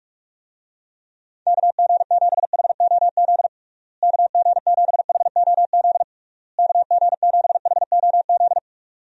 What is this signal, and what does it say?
Morse code, heard as a keyed tone, about 42 wpm, 700 Hz. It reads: KG7HOZ KG7HOZ KG7HOZ
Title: USA Call Signs - 6 Characters - Repeated 3x - 45wpm